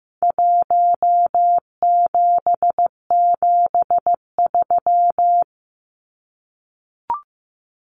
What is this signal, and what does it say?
Morse code, heard as a keyed tone, about 15 wpm, 700 Hz. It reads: 1773
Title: Numbers - Contest Style - 4-Digit 15wpm